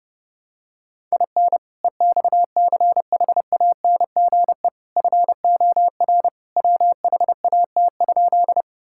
Morse code, heard as a keyed tone, about 30 wpm, 700 Hz. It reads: IN EXCHANGE FOR WHAT?
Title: Sentences from Top 1000 Words - Review Part 6 - 30wpm